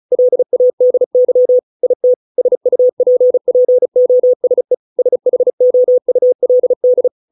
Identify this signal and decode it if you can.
Morse code, heard as a keyed tone, about 34 wpm, 500 Hz. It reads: LADY IT SUPPOSE SHOULD